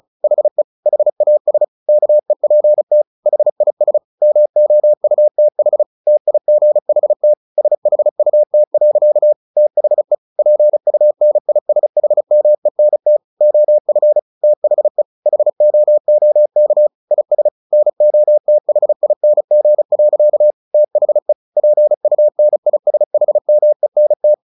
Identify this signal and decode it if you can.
Morse code, heard as a keyed tone, about 34 wpm, 600 Hz. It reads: HE HAS KEPT HIS MOUTH TIGHT SHUT. THE PUNISHMENT OF THE HOOK IS NOTHING. THE PUNISHMENT
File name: the old man and the sea 2 35wpm